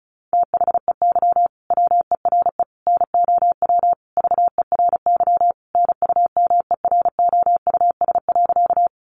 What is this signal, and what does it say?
Morse code, heard as a keyed tone, about 34 wpm, 700 Hz. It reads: THEY WERE NOW VERY NUMEROUS.
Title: Sentences from Top 1400 Words 35wpm